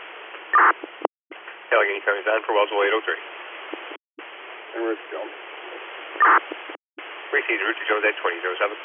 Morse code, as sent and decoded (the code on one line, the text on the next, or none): none